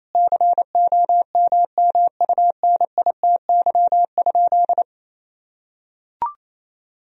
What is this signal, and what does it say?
Morse code, heard as a keyed tone, about 28 wpm, 700 Hz. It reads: COMMUNITY?
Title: Sentences from Top 1000 Words - Review Part 4 - 28wpm